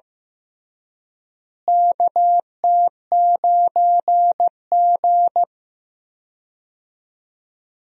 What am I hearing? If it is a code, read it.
Morse code, heard as a keyed tone, about 15 wpm, 700 Hz. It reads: KT9G
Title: USA Call Signs - 4 Characters - Repeated 2x - 15wpm